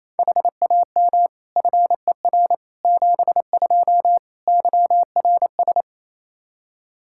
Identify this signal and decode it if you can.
Morse code, heard as a keyed tone, about 28 wpm, 700 Hz. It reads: HAM FER 72 YRS
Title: K1USN SST QSO Practice 28wpm